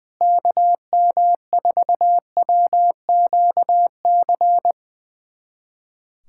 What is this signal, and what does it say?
Morse code, heard as a keyed tone, about 20 wpm, 700 Hz. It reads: KM4WQC
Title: USA Callsign DE Callsign - 6 Characters - 20wpm